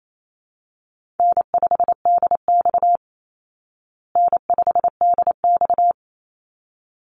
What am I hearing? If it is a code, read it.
Morse code, heard as a keyed tone, about 28 wpm, 700 Hz. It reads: N5DX N5DX
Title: Easy Call Signs Repeated 2x - 28wpm